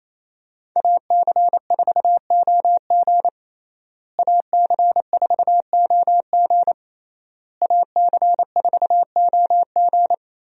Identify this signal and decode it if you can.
Morse code, heard as a keyed tone, about 28 wpm, 700 Hz. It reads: AC4OG AC4OG AC4OG
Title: USA Call Signs - 5 Characters - Repeated 3x - 28wpm